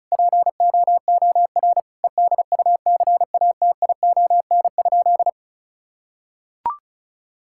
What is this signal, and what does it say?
Morse code, heard as a keyed tone, about 36 wpm, 700 Hz. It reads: POOR EDUCATION?
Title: Sentences from Top 1000 Words - Review Part 6 - 35wpm